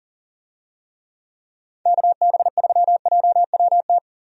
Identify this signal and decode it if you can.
Morse code, heard as a keyed tone, about 40 wpm, 700 Hz. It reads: KB3JWT
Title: USA Call Signs - 6 Characters - 40wpm